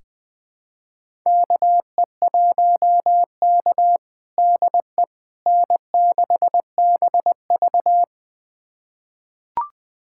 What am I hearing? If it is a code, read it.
Morse code, heard as a keyed tone, about 20 wpm, 700 Hz. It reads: KE1K DE N6BV
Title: USA Callsign DE Callsign - 4 Characters - 20wpm